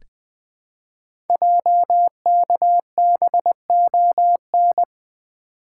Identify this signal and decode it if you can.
Morse code, heard as a keyed tone, about 20 wpm, 700 Hz. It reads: JKBON
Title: Code Groups - 5 Letter:Numbers - 20wpm